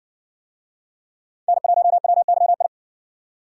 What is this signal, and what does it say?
Morse code, heard as a keyed tone, about 60 wpm, 700 Hz. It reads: N1WYI